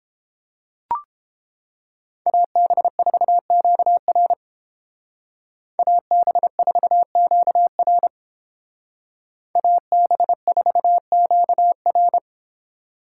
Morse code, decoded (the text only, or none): AB4QR AB4QR AB4QR